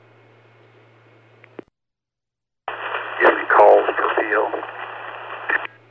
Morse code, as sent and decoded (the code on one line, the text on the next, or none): none